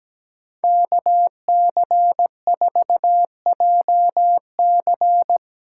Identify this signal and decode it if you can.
Morse code, heard as a keyed tone, about 17 wpm, 700 Hz. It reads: KC4JC
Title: USA Call Signs - 5 Characters - 2x Character Spacing - 17wpm